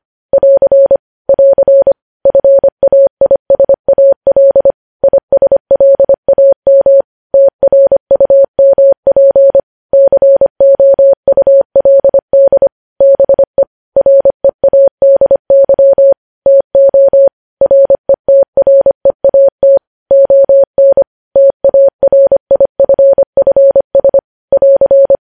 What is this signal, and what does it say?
Morse code, heard as a keyed tone, about 25 wpm, 550 Hz. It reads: <AR> <AR> FAISAL ISLAM TRUMP COULD BE READY TO RETREAT ON TARIFFS <AR>
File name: podcast / News headlines-1x-25x25